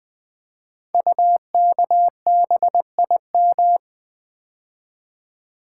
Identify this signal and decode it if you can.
Morse code, heard as a keyed tone, about 20 wpm, 700 Hz. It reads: UKBIM
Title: Code Groups - 5 Letters - 20wpm